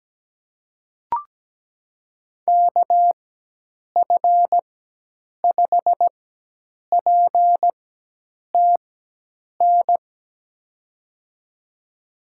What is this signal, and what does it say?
Morse code, heard as a keyed tone, about 17 wpm, spaced as for 8 wpm, 700 Hz. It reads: KF5PTN
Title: USA Call Signs - 6 Characters - 4x Character Spacing - 17wpm